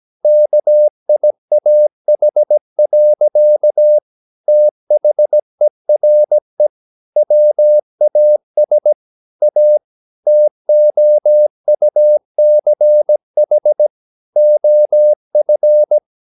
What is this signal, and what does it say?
Morse code, heard as a keyed tone, about 17 wpm, 600 Hz. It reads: KIAH. THERE WAS A TOUCH OF
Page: Romanen Never af Ken Follet på engelsk.